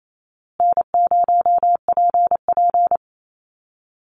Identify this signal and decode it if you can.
Morse code, heard as a keyed tone, about 28 wpm, 700 Hz. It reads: N0PP
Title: Call Signs Repeated 2x - 28wpm